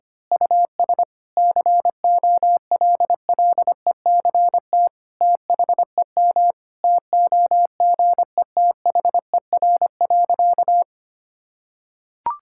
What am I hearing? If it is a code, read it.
Morse code, heard as a keyed tone, about 25 wpm, 700 Hz. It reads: US COLLECT THEM TOGETHER.
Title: Sentences from Top 1600 Words 25wpm